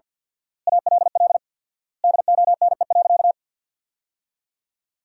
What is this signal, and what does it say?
Morse code, heard as a keyed tone, about 46 wpm, 700 Hz. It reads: ALL DONE.
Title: 4x Word-Spacing - Sentences from Top 500 Words - 50wpm